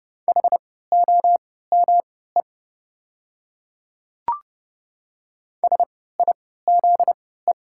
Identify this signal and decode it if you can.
Morse code, heard as a keyed tone, about 30 wpm, spaced as for 17 wpm, 690 Hz. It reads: HOME SIZE
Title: 3x Character Spacing - Top 500 Words - 30wpm